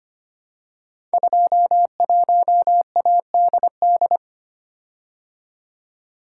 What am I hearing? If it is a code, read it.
Morse code, heard as a keyed tone, about 25 wpm, 700 Hz. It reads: 21ADD
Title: Code Groups - 5 Letter:Numbers - 25wpm